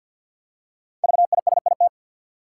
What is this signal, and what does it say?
Morse code, heard as a keyed tone, about 49 wpm, 700 Hz. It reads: VISIT
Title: Five Letter Words - 50wpm